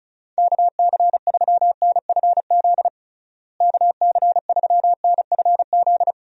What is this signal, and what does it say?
Morse code, heard as a keyed tone, about 36 wpm, 700 Hz. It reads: KC3NFZ KC3NFZ
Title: USA Call Signs - 6 Characters - Repeated 3x - 35wpm